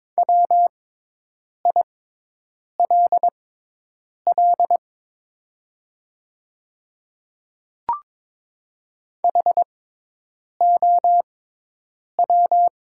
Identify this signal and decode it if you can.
Morse code, heard as a keyed tone, about 22 wpm, spaced as for 8 wpm, 700 Hz. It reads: WILL HOW
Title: 6x Character Spacing - Top 100 Words - 22wpm